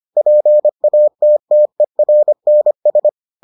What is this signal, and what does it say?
Morse code, heard as a keyed tone, about 25 wpm, 600 Hz. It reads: PATTERNS